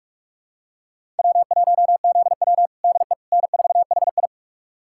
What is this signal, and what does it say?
Morse code, heard as a keyed tone, about 42 wpm, 700 Hz. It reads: W1ZW DE N4HI
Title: USA Callsign DE Callsign - 4 Characters - 45wpm